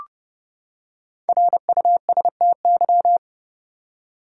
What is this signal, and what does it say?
Morse code, heard as a keyed tone, about 32 wpm, 700 Hz. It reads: RUSTY
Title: USA Common Names - Max 5 Letters - 30wpm